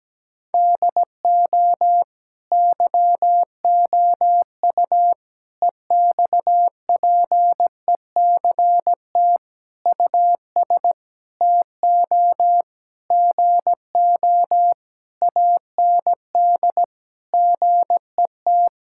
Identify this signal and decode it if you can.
Morse code, heard as a keyed tone, about 17 wpm, 700 Hz. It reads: DO YOU EXPECT US TO GO AND GET
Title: Sentences from Top 1600 Words 17wpm